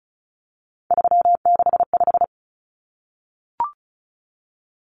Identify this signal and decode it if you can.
Morse code, heard as a keyed tone, about 34 wpm, 700 Hz. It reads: 365
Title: Common Cut Numbers - 3-Digit 35wpm